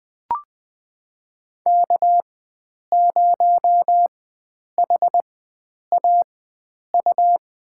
Morse code, heard as a keyed tone, about 20 wpm, spaced as for 9 wpm, 700 Hz. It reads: K0HAU
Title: USA Call Signs - 5 Characters - 4x Character Spacing - 20wpm